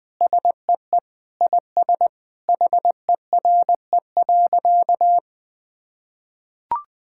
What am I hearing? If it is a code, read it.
Morse code, heard as a keyed tone, about 21 wpm, 700 Hz. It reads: SEE IS HERE.
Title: Sentences with 4-Word Phrases - 20wpm